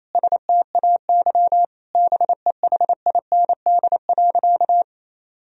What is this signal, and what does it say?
Morse code, heard as a keyed tone, about 28 wpm, 700 Hz. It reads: STAY BEHIND.
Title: Sentences from Top 1000 Words - Review Part 1 - 28wpm